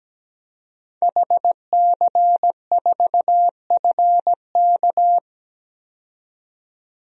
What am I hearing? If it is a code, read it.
Morse code, heard as a keyed tone, about 17 wpm, 700 Hz. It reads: HC4FK
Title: DXCC-100 Call Signs - 5 Characters - 17wpm